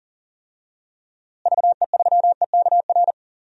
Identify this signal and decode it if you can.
Morse code, heard as a keyed tone, about 40 wpm, 700 Hz. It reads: VE3EKR
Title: DXCC-100 Call Signs - 6 Characters - Repeated 3x - 40wpm